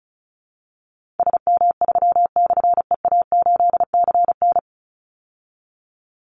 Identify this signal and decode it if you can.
Morse code, heard as a keyed tone, about 34 wpm, 700 Hz. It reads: SM3/EA8CN
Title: Call Signs Repeated 2x - 35wpm